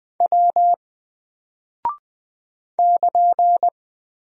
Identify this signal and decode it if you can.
Morse code, heard as a keyed tone, about 20 wpm, 700 Hz. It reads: W <KN>
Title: Single Character (Letter:Number:Punctuation:Prosign) 20wpm